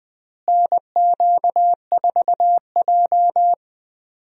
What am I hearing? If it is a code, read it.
Morse code, heard as a keyed tone, about 20 wpm, 700 Hz. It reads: NQ4J
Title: USA Call Signs - 4 Characters - Repeated 3x - 20wpm